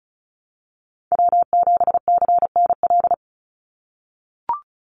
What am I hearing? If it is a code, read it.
Morse code, heard as a keyed tone, about 34 wpm, 700 Hz. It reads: W7CNL
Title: NA SOTA Chaser Callsigns 35wpm